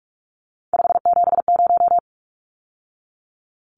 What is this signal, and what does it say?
Morse code, heard as a keyed tone, about 44 wpm, 700 Hz. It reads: 570
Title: Numbers - 3-Digit 45wpm